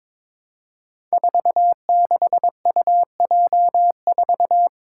Morse code, heard as a keyed tone, about 22 wpm, 700 Hz. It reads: 46UJ4
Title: Five Challenging Letter-Number Combinations 22wpm